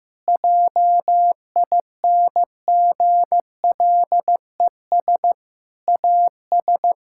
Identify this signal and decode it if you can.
Morse code, heard as a keyed tone, about 15 wpm, 700 Hz. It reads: JINGLES AS
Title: Pangram Challenge v2 - 15wpm